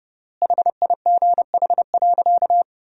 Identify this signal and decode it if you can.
Morse code, heard as a keyed tone, about 30 wpm, 700 Hz. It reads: HIGH.
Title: 3x Word-Spacing - Sentences from Top 200 Words - 30wpm